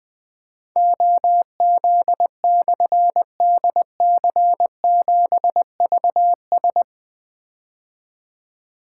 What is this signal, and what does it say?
Morse code, heard as a keyed tone, about 20 wpm, 700 Hz. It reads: OZ/DC7VS
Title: Difficult Call Signs 20wpm